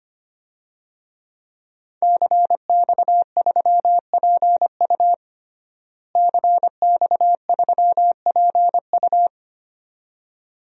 Morse code, heard as a keyed tone, about 25 wpm, 700 Hz. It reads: CX3PU CX3PU
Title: DXCC-100 Call Signs - 5 Characters - Repeated 2x - 25wpm